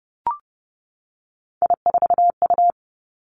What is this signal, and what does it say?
Morse code, heard as a keyed tone, about 30 wpm, 700 Hz. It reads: I4U
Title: Three Challenging Letter-Number Combinations 30wpm